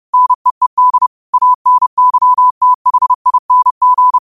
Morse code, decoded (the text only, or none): NEED ANYTHING